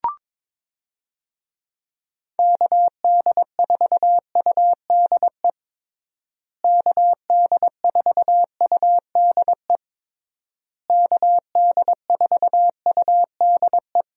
Morse code, heard as a keyed tone, about 22 wpm, 700 Hz. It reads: KD4UDE KD4UDE KD4UDE
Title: USA Call Signs - 6 Characters - Repeated 3x - 22wpm